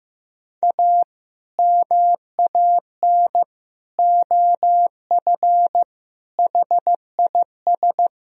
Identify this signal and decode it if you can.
Morse code, heard as a keyed tone, about 15 wpm, 700 Hz. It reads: A MAN OF HIS
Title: Sentences from Top 1000 Words - Review Part 3 - 15wpm